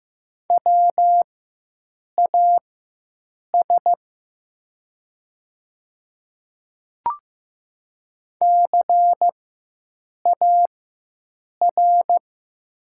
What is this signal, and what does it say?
Morse code, heard as a keyed tone, about 15 wpm, spaced as for 7 wpm, 700 Hz. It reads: WAS CAR
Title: 4x Character Spacing - Top 300 Words - 15wpm